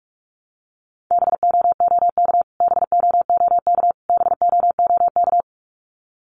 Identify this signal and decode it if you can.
Morse code, heard as a keyed tone, about 44 wpm, 700 Hz. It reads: BOOK BOOK BOOK
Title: Top 500 Words Repeated 45wpm